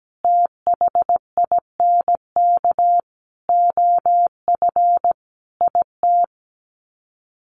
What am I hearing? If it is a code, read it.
Morse code, heard as a keyed tone, about 17 wpm, 700 Hz. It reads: THINK OF IT
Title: Sets of 5 Words Encore 17wpm